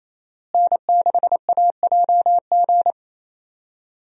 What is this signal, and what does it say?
Morse code, heard as a keyed tone, about 27 wpm, 700 Hz. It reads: N6AJG
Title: USA Call Signs - 5 Characters - Repeated 2x - 28wpm